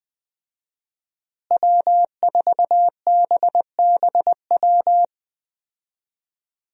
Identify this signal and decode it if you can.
Morse code, heard as a keyed tone, about 20 wpm, 700 Hz. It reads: W4BBW
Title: USA Call Signs - 5 Characters - Repeated 3x - 20wpm